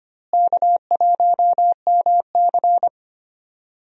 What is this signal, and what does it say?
Morse code, heard as a keyed tone, about 25 wpm, 700 Hz. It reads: K1MC
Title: USA Call Signs - 4 Characters - Repeated 3x - 25wpm